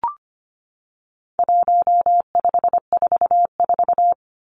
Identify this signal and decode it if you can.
Morse code, heard as a keyed tone, about 25 wpm, 700 Hz. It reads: 1544
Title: Numbers - 4-Digit 25wpm